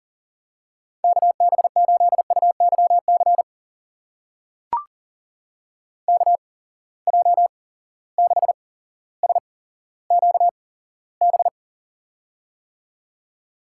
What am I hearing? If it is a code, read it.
Morse code, heard as a keyed tone, about 40 wpm, spaced as for 17 wpm, 700 Hz. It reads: KB8UYC KJ6SQB
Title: USA Call Signs - 6 Characters - 8x Character Spacing - 40wpm